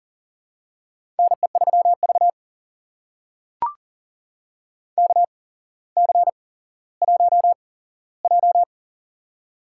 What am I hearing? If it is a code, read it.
Morse code, heard as a keyed tone, about 40 wpm, spaced as for 17 wpm, 700 Hz. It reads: NE3V KC1J
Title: USA Call Signs - 4 Characters - 8x Character Spacing - 40wpm